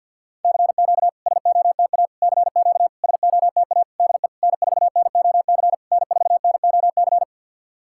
Morse code, heard as a keyed tone, about 49 wpm, 700 Hz. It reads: CQ SOTA CQ SOTA DE N4NOC N4NOC
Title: SOTA QSO Practice 50wpm